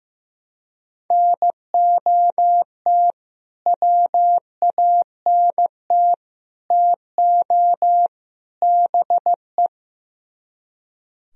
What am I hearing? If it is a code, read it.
Morse code, heard as a keyed tone, about 15 wpm, 700 Hz. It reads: NOT WANT TO BE
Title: Speed-Racing - Sets of 4 Words 15wpm